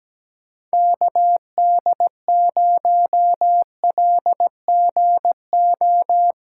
Morse code, heard as a keyed tone, about 17 wpm, 700 Hz. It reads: KD0LGO
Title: USA Call Signs - 6 Characters - 8x Character Spacing - 17wpm